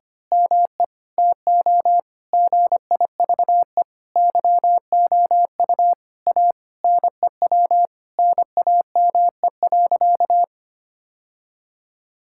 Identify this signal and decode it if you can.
Morse code, heard as a keyed tone, about 25 wpm, 700 Hz. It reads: ME TO GIVE YOU A NEW NAME.
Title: Sentences from Top 1000 Words - Review Part 5 - 25wpm